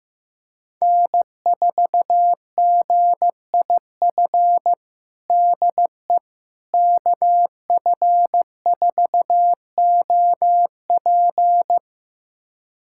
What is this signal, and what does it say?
Morse code, heard as a keyed tone, about 15 wpm, 700 Hz. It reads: N4GIF DE KF4OP
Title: USA Callsign DE Callsign - 5 Characters - 15wpm